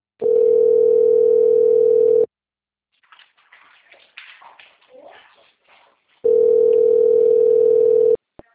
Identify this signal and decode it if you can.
Ringback tone (North America), 440 + 480 Hz.